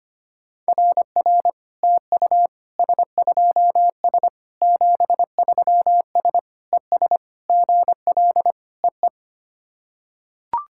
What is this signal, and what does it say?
Morse code, heard as a keyed tone, about 25 wpm, 700 Hz. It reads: RR TU S2S 73S ES GL EE
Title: SOTA QSO Practice 25wpm